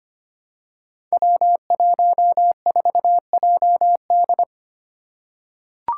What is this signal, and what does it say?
Morse code, heard as a keyed tone, about 25 wpm, 700 Hz. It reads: W14JD
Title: Five Challenging Letter-Number Combinations 25wpm